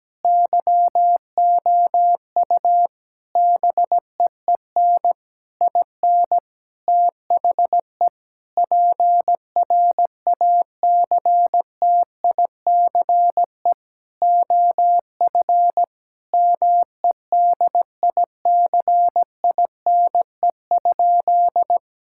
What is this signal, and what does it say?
Morse code, heard as a keyed tone, about 17 wpm, 700 Hz. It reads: YOU BEEN IN THE PRACTICE OF MEDICINE?
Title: Sentences from Top 1000 Words - Review Part 6 - 17wpm